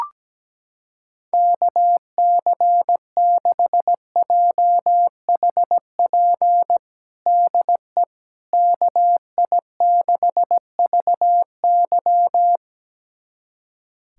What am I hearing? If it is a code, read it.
Morse code, heard as a keyed tone, about 17 wpm, 700 Hz. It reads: KC6JHP DE KI6VY
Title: USA Callsign DE Callsign 17wpm